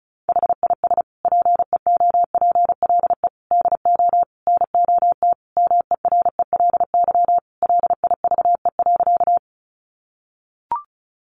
Morse code, heard as a keyed tone, about 34 wpm, 700 Hz. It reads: HIS PEOPLE DO NOT MERELY LIVE.
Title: Sentences from Top 1400 Words 35wpm